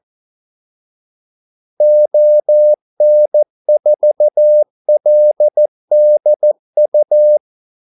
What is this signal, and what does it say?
Morse code, heard as a keyed tone, about 14 wpm, 600 Hz. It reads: ON4LDU
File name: Callsigns-14wpm